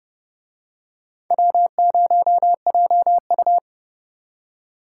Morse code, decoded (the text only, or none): W0JU